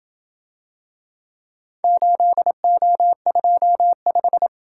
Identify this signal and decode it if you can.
Morse code, heard as a keyed tone, about 28 wpm, 700 Hz. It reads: 8O25